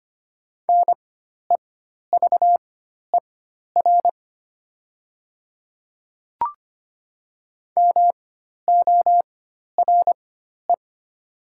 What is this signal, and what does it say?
Morse code, heard as a keyed tone, about 25 wpm, spaced as for 12 wpm, 700 Hz. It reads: NEVER MORE
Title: 4x Character Spacing - Top 500 Words - 25wpm